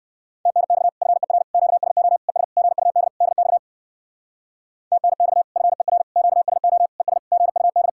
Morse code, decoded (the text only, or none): TNX FER QSO ES GUD DX TNX FER QSO ES GUD